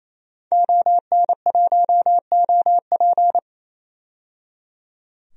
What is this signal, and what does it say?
Morse code, heard as a keyed tone, about 28 wpm, 700 Hz. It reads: ON1OP
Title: DXCC-100 Call Signs - 5 Characters - 28wpm